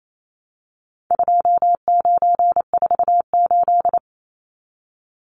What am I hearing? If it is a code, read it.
Morse code, heard as a keyed tone, about 28 wpm, 700 Hz. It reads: 2948
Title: Numbers - 4-Digit 28wpm